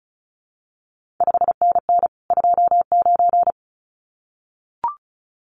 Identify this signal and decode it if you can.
Morse code, heard as a keyed tone, about 36 wpm, 700 Hz. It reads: 5NN 29
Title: CQ WW DX Contest Exchange Encore - 35wpm